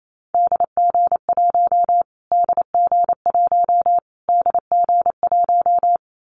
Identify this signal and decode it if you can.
Morse code, heard as a keyed tone, about 28 wpm, 700 Hz. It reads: DG1 DG1 DG1
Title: Callsign Prefixes for Top 25 DXCC - Repeated 3x - 28wpm